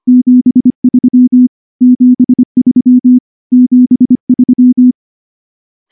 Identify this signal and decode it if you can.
Morse code, heard as a keyed tone, about 24 wpm, 250 Hz. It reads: 73 73 73
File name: morse-training-25WPM